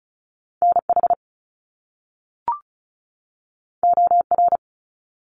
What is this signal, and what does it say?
Morse code, heard as a keyed tone, about 36 wpm, 700 Hz. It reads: NH OR
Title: US States 35wpm